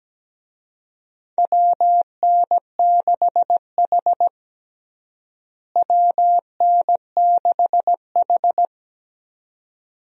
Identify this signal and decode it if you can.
Morse code, heard as a keyed tone, about 17 wpm, 700 Hz. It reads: WN6H WN6H
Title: USA Call Signs - 4 Characters - Repeated 2x - 17wpm